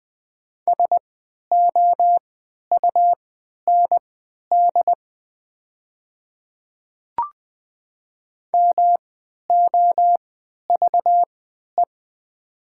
Morse code, heard as a keyed tone, about 20 wpm, spaced as for 11 wpm, 700 Hz. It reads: SOUND MOVE
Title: 3x Character Spacing - Top 500 Words - 20wpm